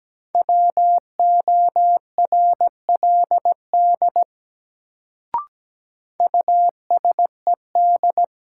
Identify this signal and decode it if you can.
Morse code, heard as a keyed tone, about 17 wpm, 700 Hz. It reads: WORLD USED
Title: Sets of 2 Words - New Word Drill - 17wpm